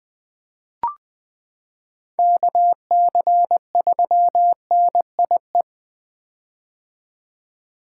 Morse code, heard as a keyed tone, about 20 wpm, 700 Hz. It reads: KC3NIE
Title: K1USN SST QSO Practice - 20wpm